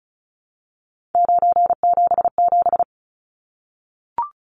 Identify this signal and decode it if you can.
Morse code, heard as a keyed tone, about 34 wpm, 700 Hz. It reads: 977
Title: Numbers - 3-Digit 35wpm